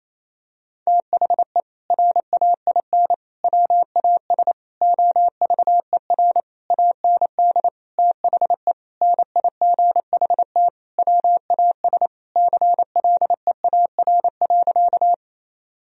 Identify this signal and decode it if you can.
Morse code, heard as a keyed tone, about 28 wpm, 700 Hz. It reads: THE RAIN WAS OVER AND THE NIGHT WAS CLEAR.
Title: Sentences from Top 1000 Words - Review Part 1 - 28wpm